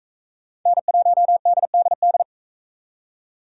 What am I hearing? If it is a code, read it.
Morse code, heard as a keyed tone, about 39 wpm, 700 Hz. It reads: N1DDD